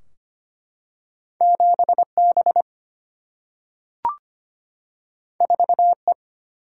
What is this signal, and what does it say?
Morse code, heard as a keyed tone, about 25 wpm, 700 Hz. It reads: 7B 4E
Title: Single-Letter Single-Number 25wpm